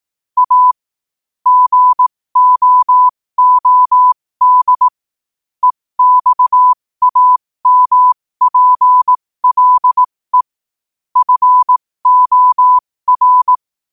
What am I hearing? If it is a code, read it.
Morse code, heard as a keyed tone, about 18 wpm, spaced as for 15 wpm, 1000 Hz. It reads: A GOOD EXAMPLE FOR
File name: qotdfiles / 15wpm